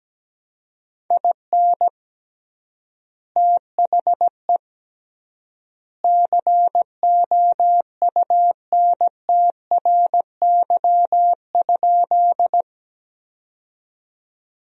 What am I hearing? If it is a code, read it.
Morse code, heard as a keyed tone, about 17 wpm, 700 Hz. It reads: IN THE COUNTRY?
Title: 3x Word-Spacing - Sentences from Top 300 Words - 17wpm